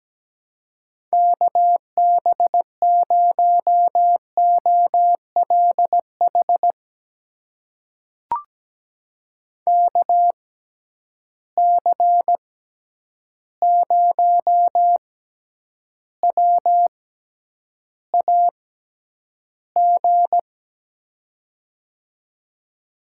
Morse code, heard as a keyed tone, about 17 wpm, spaced as for 9 wpm, 700 Hz. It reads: KB0OLH KC0WAG
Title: USA Call Signs - 6x Character Spacing - 17wpm